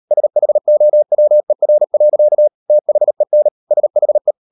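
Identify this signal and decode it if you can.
Morse code, heard as a keyed tone, about 38 wpm, 600 Hz. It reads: SHOWER. THEN SHE
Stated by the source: Morse code practice at 38 WPM, sending Ken Follett's novel Never in English, from CW med Gnister.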